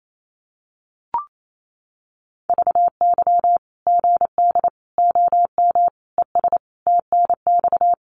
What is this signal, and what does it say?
Morse code, heard as a keyed tone, about 28 wpm, 700 Hz. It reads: VY GD OM ES TNX
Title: Common QSO Phrases 28wpm